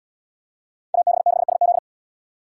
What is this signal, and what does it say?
Morse code, heard as a keyed tone, about 71 wpm, 700 Hz. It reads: AB6SZ